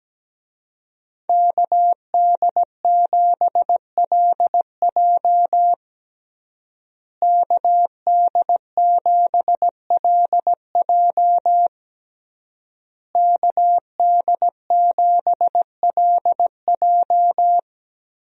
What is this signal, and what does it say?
Morse code, heard as a keyed tone, about 17 wpm, 700 Hz. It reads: KD7LJ KD7LJ KD7LJ
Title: USA Call Signs - 5 Characters - Repeated 3x - 17wpm